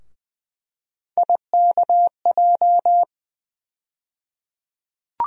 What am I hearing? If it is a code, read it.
Morse code, heard as a keyed tone, about 20 wpm, 700 Hz. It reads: IKJ
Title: USA Call Sign Postfixes - 3 Letters 20wpm